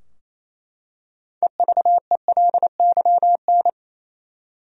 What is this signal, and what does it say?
Morse code, heard as a keyed tone, about 28 wpm, 700 Hz. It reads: EVELYN
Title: USA Common Names - 28wpm